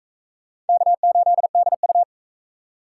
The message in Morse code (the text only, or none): K8DU